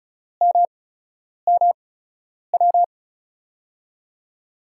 Morse code, heard as a keyed tone, about 34 wpm, spaced as for 10 wpm, 700 Hz. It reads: MMW
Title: USA Call Signs - 6 Characters - 8x Character Spacing - 35wpm